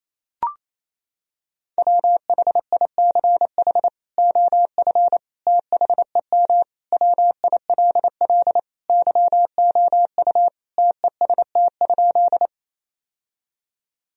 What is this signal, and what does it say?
Morse code, heard as a keyed tone, about 28 wpm, 700 Hz. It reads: WHICH OF THEM WILL YOU TEST?
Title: Sentences from Top 1000 Words - Review Part 3 - 28wpm